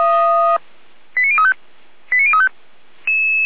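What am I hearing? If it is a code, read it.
Five-tone selective call (ZVEI), heard as quick runs of five tones, tones of about 80 ms. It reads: 89035 89035
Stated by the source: Five-tone ZVEI selective call received by SDR.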